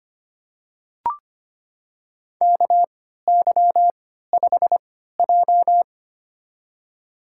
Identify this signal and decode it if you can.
Morse code, heard as a keyed tone, about 25 wpm, spaced as for 14 wpm, 700 Hz. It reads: KY5J
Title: USA Call Signs - 4 Characters - 3x Character Spacing - 25wpm